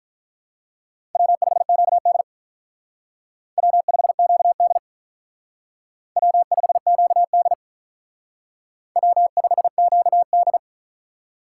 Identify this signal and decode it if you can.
Morse code, heard as a keyed tone, about 44 wpm, 700 Hz. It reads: W5QD W5QD W5QD W5QD